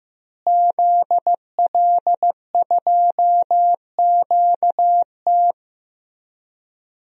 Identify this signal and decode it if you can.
Morse code, heard as a keyed tone, about 15 wpm, 700 Hz. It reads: ZL2QT
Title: DXCC-100 Call Signs - 5 Characters - Repeated 3x - 15wpm